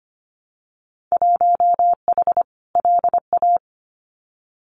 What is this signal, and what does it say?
Morse code, heard as a keyed tone, about 25 wpm, 700 Hz. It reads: 1H LA
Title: Winter Field Day Exchanges 25wpm